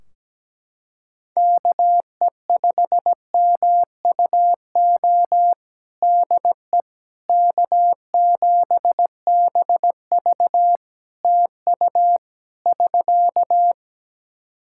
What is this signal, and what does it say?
Morse code, heard as a keyed tone, about 17 wpm, 700 Hz. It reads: KE5MUO DE K7BV TU <SK>
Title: K1USN SST QSO Practice 17wpm